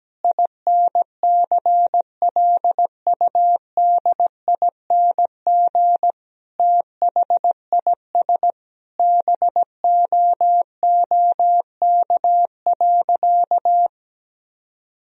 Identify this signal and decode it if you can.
Morse code, heard as a keyed tone, about 17 wpm, 700 Hz. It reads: INCLUDING THIS BOOK.
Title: Sentences from Top 1000 Words - Review Part 4 - 17wpm